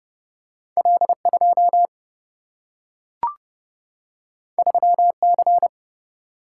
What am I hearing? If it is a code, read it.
Morse code, heard as a keyed tone, about 30 wpm, 700 Hz. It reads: L 2 3 C
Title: Single-Letter Single-Number 30wpm